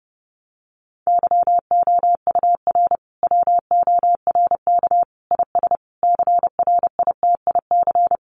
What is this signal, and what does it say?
Morse code, heard as a keyed tone, about 30 wpm, 700 Hz. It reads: YOUR WORK IS CRITIC
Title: Sentences from Top 1100 Words 30wpm